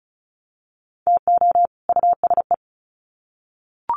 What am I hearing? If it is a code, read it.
Morse code, heard as a keyed tone, about 34 wpm, 700 Hz. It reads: TO USE
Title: Sets of 2 Words 35wpm